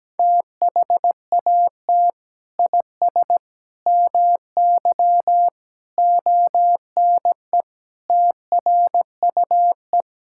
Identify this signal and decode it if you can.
Morse code, heard as a keyed tone, about 17 wpm, 700 Hz. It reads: THAT IS MY ONE TRUE
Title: Sentences from Top 1000 Words - Review Part 5 - 17wpm